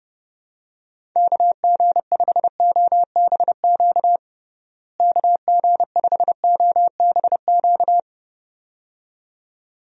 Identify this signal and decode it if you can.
Morse code, heard as a keyed tone, about 30 wpm, 700 Hz. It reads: KG5OBQ KG5OBQ
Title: USA Call Signs - 6 Characters - Repeated 2x - 30wpm